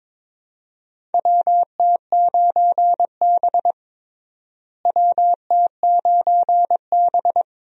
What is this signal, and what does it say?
Morse code, heard as a keyed tone, about 22 wpm, 700 Hz. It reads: WT9B WT9B
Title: USA Call Signs - 4 Characters - Repeated 3x - 22wpm